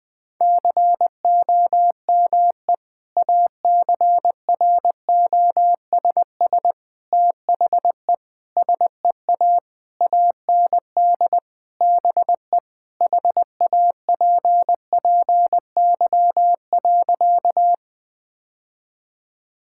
Morse code, heard as a keyed tone, about 20 wpm, 700 Hz. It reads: COME ACROSS THE SEA AND BE HAPPY.
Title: Sentences from Top 1000 Words - Review Part 5 - 20wpm